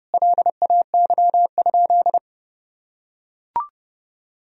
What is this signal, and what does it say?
Morse code, heard as a keyed tone, about 30 wpm, 700 Hz. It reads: LAY?
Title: Sentences from Top 400 Words and Abbreviations 30wpm